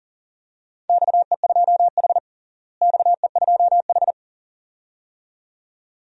Morse code, heard as a keyed tone, about 40 wpm, 700 Hz. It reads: XE2H XE2H
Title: DXCC-100 Call Signs - 4 Characters - Repeated 2x - 40wpm